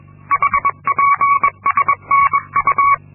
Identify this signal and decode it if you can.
Morse code, heard as a keyed tone, about 22 wpm, 1120 Hz. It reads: HPSNU
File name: Linksmakalnis-2